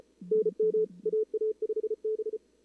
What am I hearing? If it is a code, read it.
Morse code, heard as a keyed tone, about 36 wpm, 440 Hz. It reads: NM AA5B
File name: CWSprint / Feb2022 / k5tr-feb-na-sprint-cw-k5ot -0000